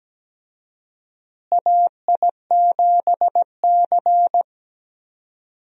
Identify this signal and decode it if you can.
Morse code, heard as a keyed tone, about 17 wpm, 700 Hz. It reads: AI7C
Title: USA Call Signs - 4 Characters - Repeated 2x - 17wpm